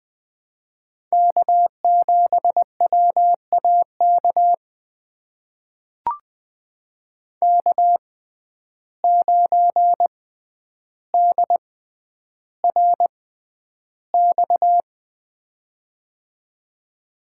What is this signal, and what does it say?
Morse code, heard as a keyed tone, about 20 wpm, spaced as for 10 wpm, 700 Hz. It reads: K7WAK K9DRX
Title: USA Call Signs - 5 Characters - 6x Character Spacing - 20wpm